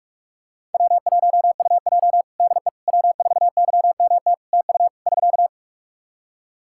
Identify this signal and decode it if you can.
Morse code, heard as a keyed tone, about 46 wpm, 700 Hz. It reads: W1UJ DE W4YMT TU <SK>
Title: Practice QSO - Exchange 4 - Ending - 45wpm